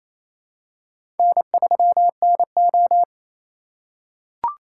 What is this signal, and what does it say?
Morse code, heard as a keyed tone, about 28 wpm, 700 Hz. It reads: N3NO
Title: USA Call Signs - 4 Characters - 2x Character Spacing - 28wpm